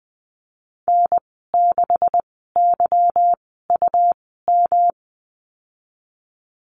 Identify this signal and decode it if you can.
Morse code, heard as a keyed tone, about 19 wpm, spaced as for 14 wpm, 700 Hz. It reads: N6YUM
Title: 154 Lesson 25g - Letter p - Call Signs 20wpm